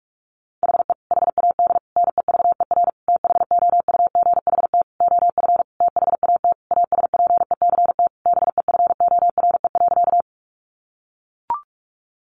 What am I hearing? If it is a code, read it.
Morse code, heard as a keyed tone, about 44 wpm, 700 Hz. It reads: HE HAD NEVER THOUGHT OF THAT ASPECT BEFORE.
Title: Sentences from Top 1300 Words 45wpm